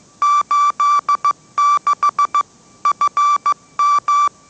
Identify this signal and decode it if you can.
Morse code, heard as a keyed tone, about 15 wpm, 1160 Hz. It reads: 86 FM
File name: mrx / mrx065 16wpm